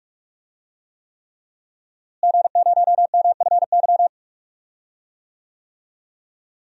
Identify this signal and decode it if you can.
Morse code, heard as a keyed tone, about 46 wpm, 700 Hz. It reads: G0MFY
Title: DXCC-100 Call Signs - 5 Characters - Repeated 3x - 45wpm